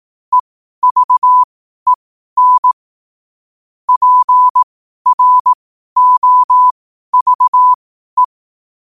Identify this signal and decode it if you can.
Morse code, heard as a keyed tone, about 18 wpm, spaced as for 12 wpm, 1000 Hz. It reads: EVEN PROVE
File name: qotdfiles / 12wpm